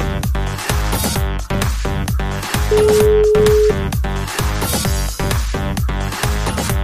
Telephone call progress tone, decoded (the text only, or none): ringback tone